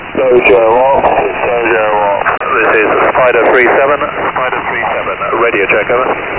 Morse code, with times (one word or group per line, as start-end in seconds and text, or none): none